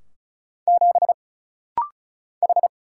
Morse code, heard as a keyed tone, about 33 wpm, 700 Hz. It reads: 7 H
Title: Single Letter-Number - Rapid-Fire 35wpm